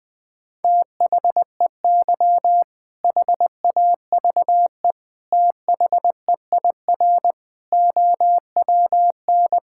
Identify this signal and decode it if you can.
Morse code, heard as a keyed tone, about 20 wpm, 700 Hz. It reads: THEY HAVE THEIR OWN
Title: Sentences with 2-Word Phrases - 20wpm